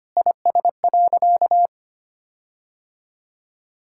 Morse code, heard as a keyed tone, about 25 wpm, 700 Hz. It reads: IS.
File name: practice / 25wpm / Sentences with 9-Word Phrases - 25wpm